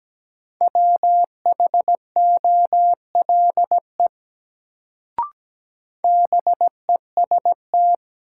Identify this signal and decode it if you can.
Morse code, heard as a keyed tone, about 17 wpm, 700 Hz. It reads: WHOLE BEST
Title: Sentences from Top 400 Words - New Word Drill - 17wpm